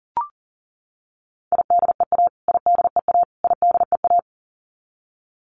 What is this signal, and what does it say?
Morse code, heard as a keyed tone, about 40 wpm, 700 Hz. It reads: IDEA IDEA IDEA
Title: Top 300 Words Repeated 40wpm